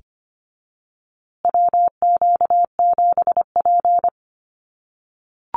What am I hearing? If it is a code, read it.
Morse code, heard as a keyed tone, about 25 wpm, 700 Hz. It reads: WQ7P
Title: USA Call Signs Repeated 2x 25wpm